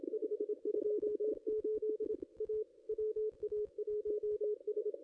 Morse code, heard as a keyed tone, about 28 wpm, 440 Hz. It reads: 528 A WA1S
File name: contests / ss-cw-k5tr-op-n5rz -2200